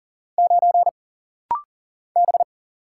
Morse code, heard as a keyed tone, about 39 wpm, 700 Hz. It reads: 9 B
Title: Single Letter-Number Alt - Rapid-Fire 40wpm